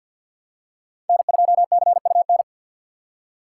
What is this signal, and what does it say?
Morse code, heard as a keyed tone, about 46 wpm, 700 Hz. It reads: N2CUN
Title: USA Call Signs - 5 Characters - 4x Character Spacing - 50wpm